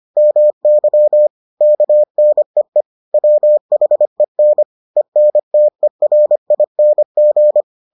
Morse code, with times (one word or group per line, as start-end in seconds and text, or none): 0.2-1.3 MY
1.6-2.8 KNEE
3.1-4.6 WHEN
5.0-7.6 ENTERING